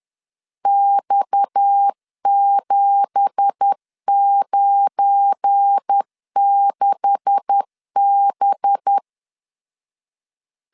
Morse code, heard as a keyed tone, about 10 wpm, 800 Hz. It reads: X796B
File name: LEZIONI / 9 2867 10wpm 2733